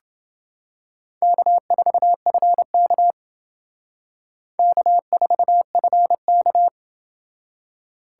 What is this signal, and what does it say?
Morse code, heard as a keyed tone, about 28 wpm, 700 Hz. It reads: K4FK K4FK